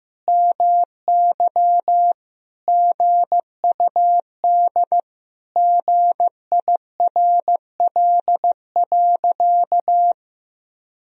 Morse code, heard as a keyed tone, about 16 wpm, 700 Hz. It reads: MY GUD GIRL.
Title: Sentences from Top 400 Words and Abbreviations 15wpm